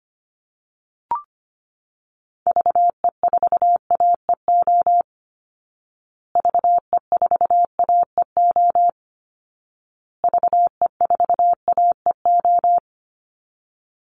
Morse code, heard as a keyed tone, about 25 wpm, 700 Hz. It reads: VE4AEO VE4AEO VE4AEO
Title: Call Signs Repeated 2x - 25wpm